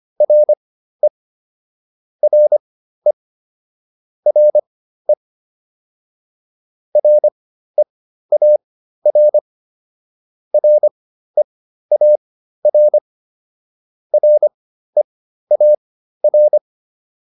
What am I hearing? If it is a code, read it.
Morse code, heard as a keyed tone, about 25 wpm, spaced as for 13 wpm, 600 Hz. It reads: RE RE RE REAR REAR REAR